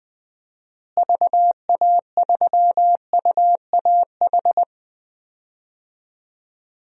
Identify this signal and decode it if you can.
Morse code, heard as a keyed tone, about 20 wpm, 700 Hz. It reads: VA3UAH
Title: DXCC-100 Call Signs - 6 Characters - Repeated 3x - 20wpm